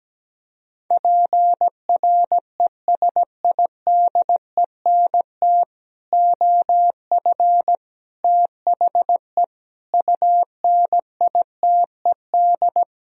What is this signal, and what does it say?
Morse code, heard as a keyed tone, about 17 wpm, 700 Hz. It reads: PRESIDENT OF THE UNITED
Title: Speed-Racing - Sets of 4 Words 17wpm